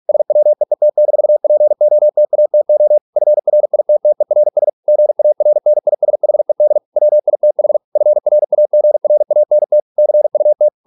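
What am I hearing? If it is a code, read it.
Morse code, heard as a keyed tone, about 46 wpm, 600 Hz. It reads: SWEET-POTATO FRITTERS GARNISHED WITH FRAGRANT CUT
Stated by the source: Morse practice at 45 wpm, the novel Never by Ken Follett in English, from CW med Gnister.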